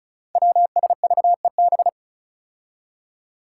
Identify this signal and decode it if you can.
Morse code, heard as a keyed tone, about 36 wpm, 700 Hz. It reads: WSVEB
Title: Five Challenging Letter-Number Combinations 35wpm